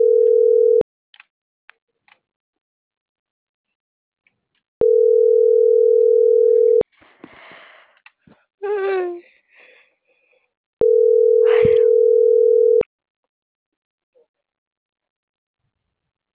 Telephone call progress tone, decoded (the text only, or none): ringback tone